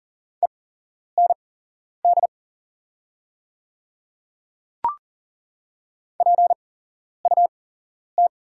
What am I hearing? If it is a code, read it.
Morse code, heard as a keyed tone, about 39 wpm, spaced as for 11 wpm, 690 Hz. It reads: END PUT
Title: 8x Character Spacing - Top 200 Words - 40wpm